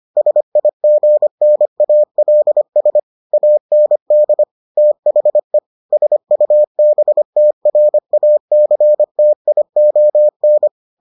Morse code, heard as a keyed tone, about 25 wpm, 600 Hz. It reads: SIGNALS AND THE SUBTRACTION